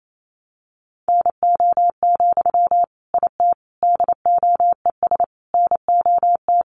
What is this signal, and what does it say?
Morse code, heard as a keyed tone, about 28 wpm, 700 Hz. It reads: NO, IT DOES NOT
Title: Sentences from Top 1000 Words 28wpm